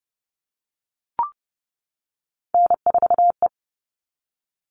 Morse code, heard as a keyed tone, about 29 wpm, 700 Hz. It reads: N4E
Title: All Cut Numbers - 3-Digit 30wpm